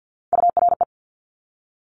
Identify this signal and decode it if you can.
Morse code, heard as a keyed tone, about 50 wpm, 700 Hz. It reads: URE
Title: Top Three Letter Combinations in Words 50wpm